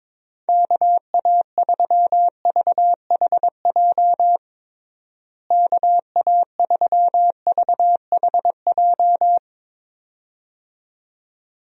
Morse code, heard as a keyed tone, about 22 wpm, 700 Hz. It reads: KA3VHJ KA3VHJ
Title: USA Call Signs - 6 Characters - Repeated 2x - 22wpm